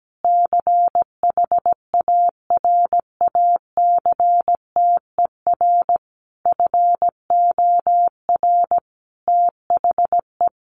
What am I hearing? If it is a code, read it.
Morse code, heard as a keyed tone, about 17 wpm, 700 Hz. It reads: CHARACTER FOR THE
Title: Sentences from Top 1200 Words 17wpm